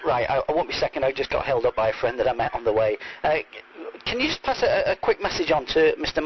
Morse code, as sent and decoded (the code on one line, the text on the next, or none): none